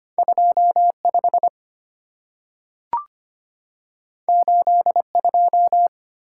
Morse code, heard as a keyed tone, about 25 wpm, 700 Hz. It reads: 25 82
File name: Two Letter-Number Combinations - 25wpm